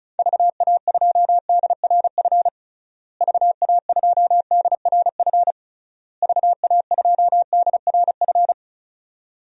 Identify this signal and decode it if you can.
Morse code, heard as a keyed tone, about 34 wpm, 700 Hz. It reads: VA2DRF VA2DRF VA2DRF
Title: DXCC-100 Call Signs - 6 Characters - Repeated 3x - 35wpm